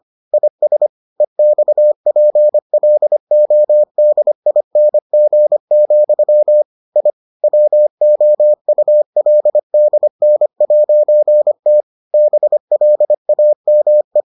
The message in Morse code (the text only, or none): IS EXPLODING, I WOULDN'T BLAME